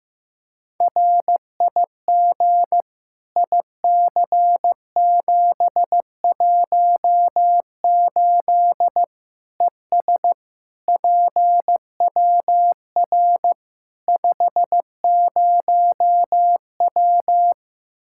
Morse code, heard as a keyed tone, about 15 wpm, 700 Hz. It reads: RIG IC718 ES PWR 50W
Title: Practice QSO - Exchange 2 - Info - 15wpm